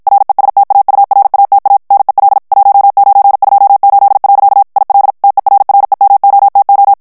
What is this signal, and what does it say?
Morse code, heard as a keyed tone, about 56 wpm, 800 Hz. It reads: RESTAURATA NEL 1928. IL TERREMOTO
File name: morse mp3 / ITALIEN texte clair 1 vitesse 50 mots minute